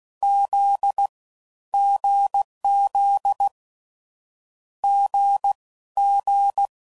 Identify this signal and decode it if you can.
Morse code, heard as a keyed tone, about 16 wpm, 800 Hz. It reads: Z GZ G G
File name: Combinations / Difficult Combos / awj1 ndb6 gz7